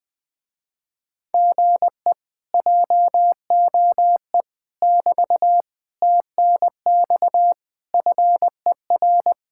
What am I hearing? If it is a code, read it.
Morse code, heard as a keyed tone, about 21 wpm, 700 Hz. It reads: GE JOE <BT> TNX FER
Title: Speed-Racing Common QSO Phrases - 20wpm